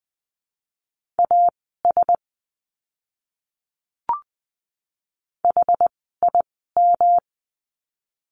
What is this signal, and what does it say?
Morse code, heard as a keyed tone, about 20 wpm, spaced as for 14 wpm, 700 Hz. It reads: AS HIM
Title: 206 Lesson 33f - Letter j - Words 20wpm